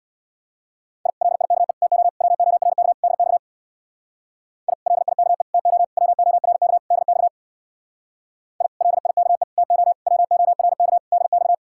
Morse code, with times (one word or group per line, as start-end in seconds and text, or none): 1.1-1.1 I
1.2-1.7 LIKE
1.8-2.1 TO
2.2-2.9 WORK
3.0-3.4 DX
4.7-4.7 I
4.9-5.4 LIKE
5.5-5.8 TO
6.0-6.8 WORK
6.9-7.3 DX
8.6-8.7 I
8.8-9.4 LIKE
9.6-9.9 TO
10.1-11.0 WORK
11.1-11.5 DX